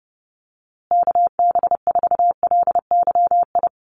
Morse code, heard as a keyed tone, about 30 wpm, 700 Hz. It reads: KB4LYI
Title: USA Call Signs Repeated 2x 30wpm